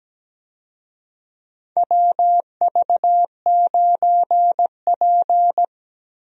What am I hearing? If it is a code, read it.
Morse code, heard as a keyed tone, about 17 wpm, 700 Hz. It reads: WV9P
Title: USA Call Signs - 4 Characters - 17wpm